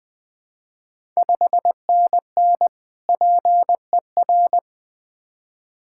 Morse code, heard as a keyed tone, about 21 wpm, 700 Hz. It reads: 5NN PER
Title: Ontario QSO Party Exchange 20wpm